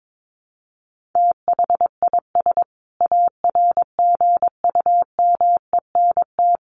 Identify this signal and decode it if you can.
Morse code, heard as a keyed tone, about 22 wpm, 700 Hz. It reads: THIS ARGUMENT
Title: Sentences from Top 1400 Words 22wpm